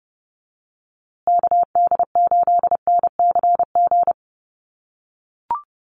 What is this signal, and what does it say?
Morse code, heard as a keyed tone, about 30 wpm, 700 Hz. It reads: KD8NCG
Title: USA Call Signs Repeated 3x 30wpm